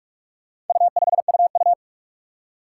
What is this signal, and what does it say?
Morse code, heard as a keyed tone, about 44 wpm, 700 Hz. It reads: U5UU
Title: Four Challenging Letter-Number Combinations 45wpm